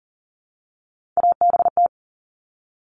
Morse code, heard as a keyed tone, about 40 wpm, 700 Hz. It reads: ABT
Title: Abbreviations Repeated 3x - 40wpm